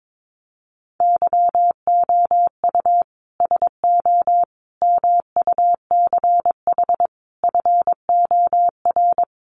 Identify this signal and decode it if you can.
Morse code, heard as a keyed tone, about 22 wpm, 700 Hz. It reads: YOU SO MUCH FOR
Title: Sets of 5 Words Encore2 22wpm